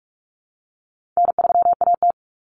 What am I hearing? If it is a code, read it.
Morse code, heard as a keyed tone, about 44 wpm, 700 Hz. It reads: N3AT
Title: Common Cut Numbers - 4-Digit 45wpm